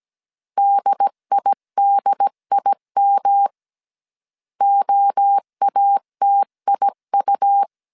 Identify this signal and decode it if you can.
Morse code, heard as a keyed tone, about 17 wpm, 800 Hz. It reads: DIDIM OATIU
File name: CW / DET / 1 3 1500C S A 16wpm 2141